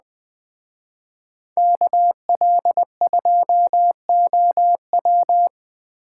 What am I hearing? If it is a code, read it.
Morse code, heard as a keyed tone, about 21 wpm, 700 Hz. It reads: KL2OW